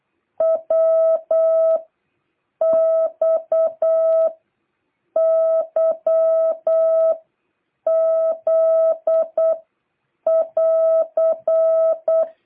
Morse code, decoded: W X Y Z <AR>

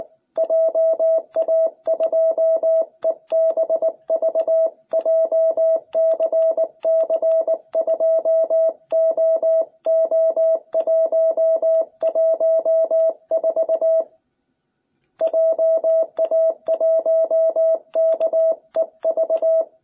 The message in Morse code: JA2EBV JCC2OO114 JA1KEV

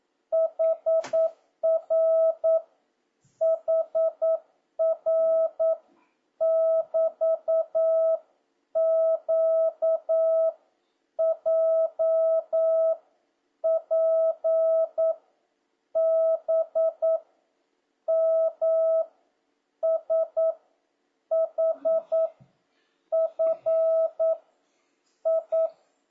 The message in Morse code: HR HR <BT>Q J P B M S H F I

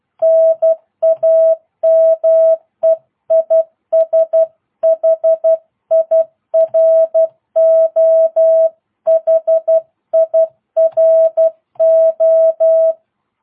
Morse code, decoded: NAME IS HIRO HIRO

